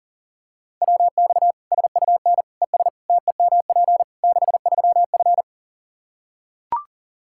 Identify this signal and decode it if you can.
Morse code, heard as a keyed tone, about 40 wpm, 700 Hz. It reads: WX SUN ES TEMP 63F
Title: Practice QSO - Exchange 2 - Info - 40wpm